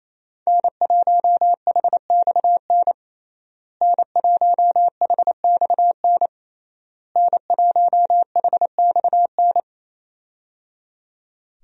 Morse code, heard as a keyed tone, about 27 wpm, 700 Hz. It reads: N1HXN N1HXN N1HXN
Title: USA Call Signs - 5 Characters - Repeated 3x - 28wpm